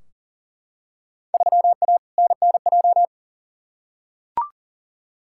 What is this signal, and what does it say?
Morse code, heard as a keyed tone, about 38 wpm, 700 Hz. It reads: 3A NNJ
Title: Field Day Exchanges - 40wpm